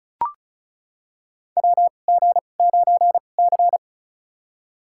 Morse code, heard as a keyed tone, about 34 wpm, spaced as for 25 wpm, 690 Hz. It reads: WG9C
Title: USA Call Signs - 4 Characters - 2x Character Spacing - 35wpm